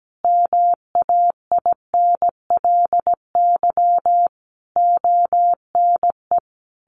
Morse code, heard as a keyed tone, about 17 wpm, 700 Hz. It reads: MAINLY ONE
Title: Sentences from Top 1400 Words 17wpm